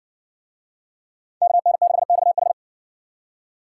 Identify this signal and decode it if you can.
Morse code, heard as a keyed tone, about 62 wpm, 700 Hz. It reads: KN6XH